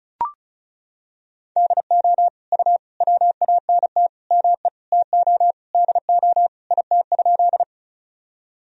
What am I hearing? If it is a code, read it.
Morse code, heard as a keyed tone, about 34 wpm, 690 Hz. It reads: DO U WANT ME TO DO IT?
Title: Sentences from Top 200 Words and Abbreviations 35wpm